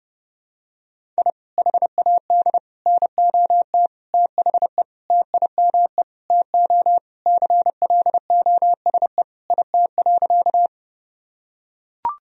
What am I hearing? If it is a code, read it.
Morse code, heard as a keyed tone, about 32 wpm, 700 Hz. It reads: I HAD NOT THE TIME TO CLOSE IT.
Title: Sentences from Top 1000 Words - Review Part 1 - 30wpm